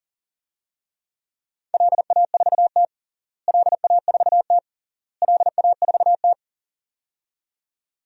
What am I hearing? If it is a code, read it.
Morse code, heard as a keyed tone, about 40 wpm, 700 Hz. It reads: LA4T LA4T LA4T
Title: DXCC-100 Call Signs - 4 Characters - Repeated 3x - 40wpm